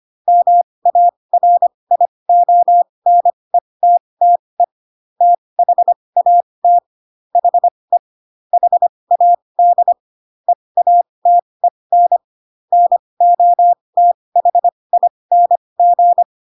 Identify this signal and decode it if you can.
Morse code, heard as a keyed tone, about 25 wpm, spaced as for 20 wpm, 700 Hz. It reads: MARIONETTE THAT HE HAD EATEN NOTHING
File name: Pinocchio / Pinocchio ch5 15 25 ik1hge